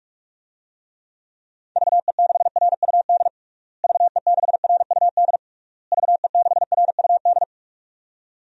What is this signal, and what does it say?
Morse code, heard as a keyed tone, about 46 wpm, 700 Hz. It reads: VE6RUD VE6RUD VE6RUD
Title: DXCC-100 Call Signs - 6 Characters - Repeated 3x - 45wpm